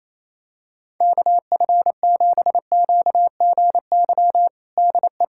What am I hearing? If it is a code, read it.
Morse code, heard as a keyed tone, about 28 wpm, 700 Hz. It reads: KF7QGY DE
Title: USA Callsign DE Callsign - 6 Characters - 28wpm